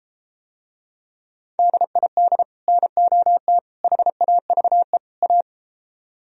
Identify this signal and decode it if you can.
Morse code, heard as a keyed tone, about 32 wpm, 700 Hz. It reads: DID NOT HAVE A